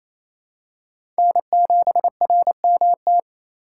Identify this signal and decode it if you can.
Morse code, heard as a keyed tone, about 28 wpm, 700 Hz. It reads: N7RMT
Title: USA Call Signs - 5 Characters - 2x Character Spacing - 28wpm